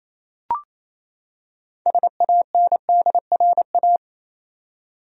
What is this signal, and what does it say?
Morse code, heard as a keyed tone, about 28 wpm, 700 Hz. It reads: SANDRA
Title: USA Common Names - 28wpm